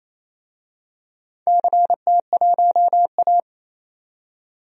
Morse code, heard as a keyed tone, about 28 wpm, 700 Hz. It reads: CT1A
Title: DXCC-100 Call Signs - 4 Characters - 28wpm